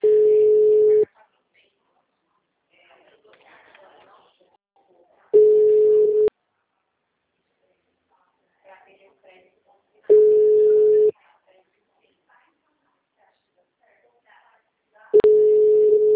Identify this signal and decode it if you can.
Ringback tone (Europe, 425 Hz), with speech.